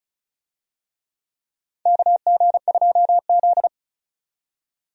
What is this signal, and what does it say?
Morse code, heard as a keyed tone, about 34 wpm, 700 Hz. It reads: KG2Z
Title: USA Call Signs - 4 Characters - 35wpm